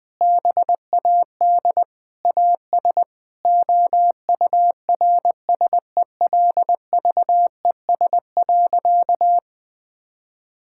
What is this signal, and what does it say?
Morse code, heard as a keyed tone, about 21 wpm, 700 Hz. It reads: BAD AS OURSELVES.
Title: Sentences from Top 1600 Words 20wpm